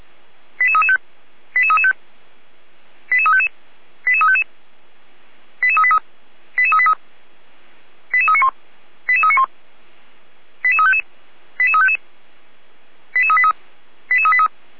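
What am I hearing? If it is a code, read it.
Five-tone selective call (ZVEI), heard as quick runs of five tones, tones of about 70 ms. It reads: 88335 88335 88355 88355 88383 88383 88391 88391 88355 88355 88383 88383
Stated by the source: ZVEI five-tone selective call received by SDR.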